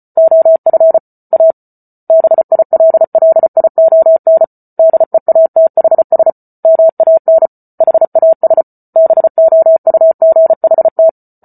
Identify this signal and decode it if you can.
Morse code, heard as a keyed tone, about 34 wpm, 650 Hz. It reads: OF A BILLION DEATHS MAN HAS BOUGHT
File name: Book Two / 08-Chapter 8 - 34 WPM